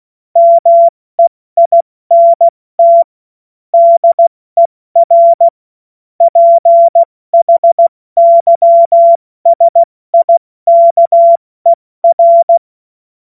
Morse code, heard as a keyed tone, about 16 wpm, 680 Hz. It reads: MEINT DER PHYSIKER
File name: Morsekurs / Witze / Witz 16. Dezember 2025 70 F80